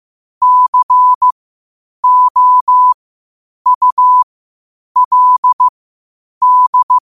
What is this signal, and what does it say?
Morse code, heard as a keyed tone, about 15 wpm, spaced as for 8 wpm, 1000 Hz. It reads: COULD
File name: qotdfiles / 8wpm